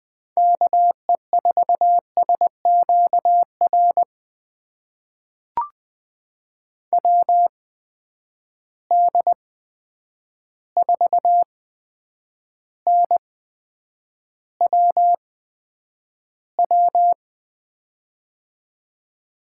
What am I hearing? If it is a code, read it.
Morse code, heard as a keyed tone, about 20 wpm, spaced as for 9 wpm, 700 Hz. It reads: KE4SQR WD4NWW
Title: USA Call Signs - 6 Characters - 8x Character Spacing - 20wpm